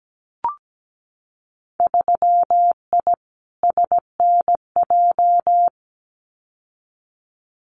Morse code, heard as a keyed tone, about 17 wpm, 700 Hz. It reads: 3I SNJ
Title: Winter Field Day Exchanges Encore 17wpm